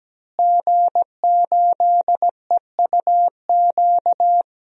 Morse code, heard as a keyed tone, about 17 wpm, 700 Hz. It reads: G8EUQ
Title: DXCC-100 Call Signs - 5 Characters - Repeated 3x - 17wpm